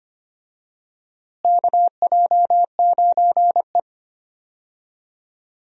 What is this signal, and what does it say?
Morse code, heard as a keyed tone, about 25 wpm, 700 Hz. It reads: KJ9E
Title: USA Call Signs - 4 Characters - Repeated 3x - 25wpm